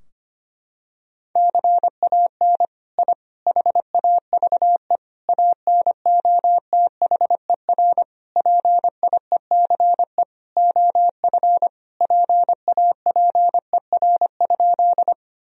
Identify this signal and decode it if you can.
Morse code, heard as a keyed tone, about 25 wpm, 700 Hz. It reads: CAN I HAVE ANOTHER PIECE OF PAPER?
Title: Sentences from Top 1000 Words - Review Part 2 - 25wpm